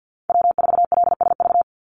Morse code, heard as a keyed tone, about 50 wpm, 700 Hz. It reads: W4LSV
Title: Call Signs Repeated 3x - 50wpm